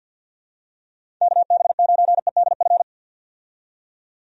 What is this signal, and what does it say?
Morse code, heard as a keyed tone, about 46 wpm, 700 Hz. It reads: KB9EDF
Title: USA Call Signs - 6 Characters - 50wpm